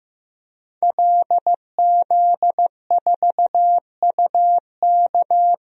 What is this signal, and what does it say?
Morse code, heard as a keyed tone, about 15 wpm, 700 Hz. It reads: LZ4UK
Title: DXCC-100 Call Signs - 5 Characters - Repeated 3x - 15wpm